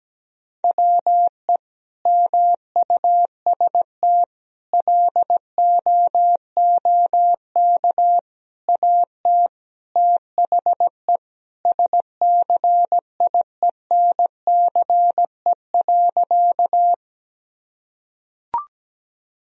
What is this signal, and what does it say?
Morse code, heard as a keyed tone, about 17 wpm, 700 Hz. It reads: WE MUST LOOK AT THE SCIENCE.
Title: Sentences from Top 1000 Words - Review Part 2 - 17wpm